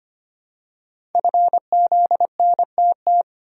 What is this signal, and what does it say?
Morse code, heard as a keyed tone, about 25 wpm, 700 Hz. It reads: FZNTT
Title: Code Groups - 5 Letters - 25wpm